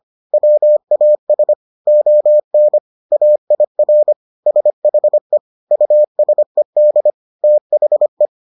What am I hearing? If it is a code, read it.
Morse code, heard as a keyed tone, about 25 wpm, 600 Hz. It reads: WAS ON AIR SHE USED THE